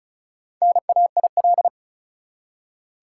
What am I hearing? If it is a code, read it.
Morse code, heard as a keyed tone, about 36 wpm, 700 Hz. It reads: NAIL
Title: Four Letter Words - 35wpm